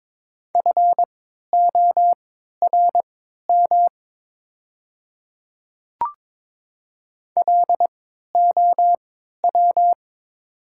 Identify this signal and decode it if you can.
Morse code, heard as a keyed tone, about 22 wpm, spaced as for 13 wpm, 700 Hz. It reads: FORM LOW
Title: 3x Character Spacing - Top 200 Words - 22wpm